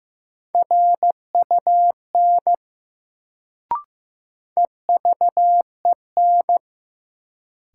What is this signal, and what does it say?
Morse code, heard as a keyed tone, about 15 wpm, 700 Hz. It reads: RUN EVEN
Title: Sets of 4 Words - New Word Drill - 15wpm